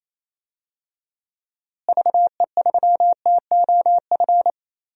Morse code, heard as a keyed tone, about 28 wpm, 700 Hz. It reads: VE3TOF
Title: DXCC-100 Call Signs - 6 Characters - Repeated 3x - 28wpm